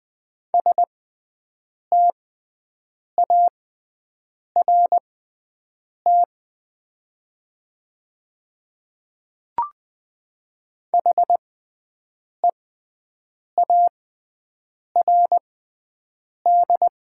Morse code, heard as a keyed tone, about 20 wpm, spaced as for 7 wpm, 700 Hz. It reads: START HEARD
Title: 6x Character Spacing - Top 500 Words - 20wpm